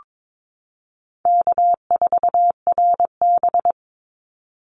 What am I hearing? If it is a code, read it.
Morse code, heard as a keyed tone, about 22 wpm, 700 Hz. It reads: K4RB
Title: Easy Call Signs Repeated 2x - 22wpm